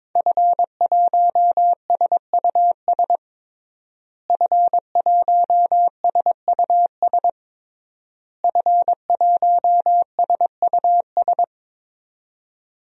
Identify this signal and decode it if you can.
Morse code, heard as a keyed tone, about 22 wpm, 700 Hz. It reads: F1SUS F1SUS F1SUS
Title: DXCC-100 Call Signs - 5 Characters - Repeated 3x - 22wpm